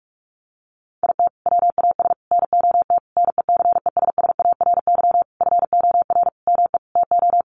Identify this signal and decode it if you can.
Morse code, heard as a keyed tone, about 44 wpm, 700 Hz. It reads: IT WAS NOT NECESSARY FOR ME TO
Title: Sentences from Top 1300 Words 45wpm